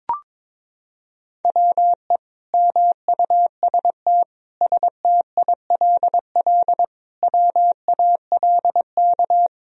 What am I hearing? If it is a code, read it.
Morse code, heard as a keyed tone, about 21 wpm, 700 Hz. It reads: WE MUST STILL WALK
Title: Sentences from Top 1000 Words - Review Part 1 - 22wpm